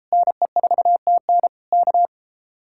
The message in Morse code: NE4TN K

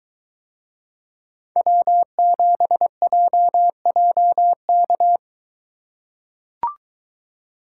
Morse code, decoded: W7JJK

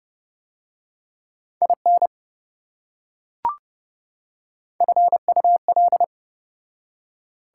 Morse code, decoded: IN FUL

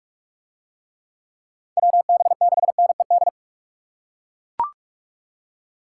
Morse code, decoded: WB6NED

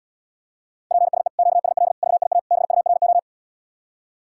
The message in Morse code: PSE QSL VIA BURO

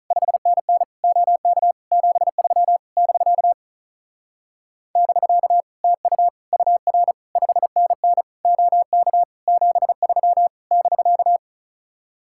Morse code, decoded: 5NN OK 73 <BK> <BK> TU UR 5NN OK 73 <BK>

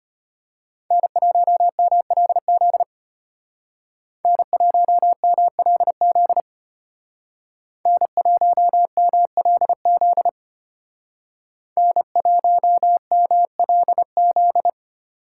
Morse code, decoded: N1MLZ N1MLZ N1MLZ N1MLZ

